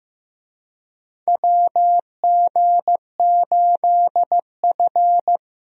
Morse code, heard as a keyed tone, about 16 wpm, 700 Hz. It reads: WG8F
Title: USA Call Signs - 4 Characters - Repeated 2x - 15wpm